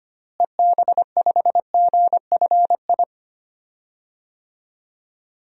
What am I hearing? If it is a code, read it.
Morse code, heard as a keyed tone, about 25 wpm, 700 Hz. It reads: EB5GFI
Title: DXCC-100 Call Signs - 6 Characters - Repeated 2x - 25wpm